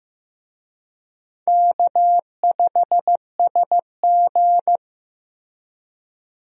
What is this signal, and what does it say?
Morse code, heard as a keyed tone, about 16 wpm, 700 Hz. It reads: K5SG
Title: USA Call Signs - 4 Characters - Repeated 3x - 15wpm